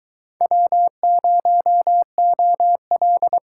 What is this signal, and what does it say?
Morse code, heard as a keyed tone, about 21 wpm, 700 Hz. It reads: W0OL